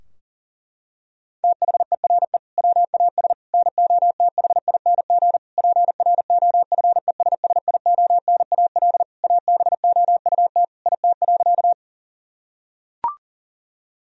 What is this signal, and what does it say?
Morse code, heard as a keyed tone, about 40 wpm, 700 Hz. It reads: THERE WAS NOTHING PROFESSIONAL ABOUT IT.
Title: Sentences from Top 1000 Words - Review Part 4 - 40wpm